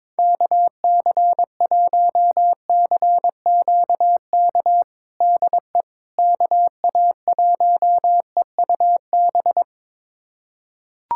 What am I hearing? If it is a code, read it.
Morse code, heard as a keyed tone, about 22 wpm, 700 Hz. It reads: KC1CQK DE KA1EUB
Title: USA Callsign DE Callsign 22wpm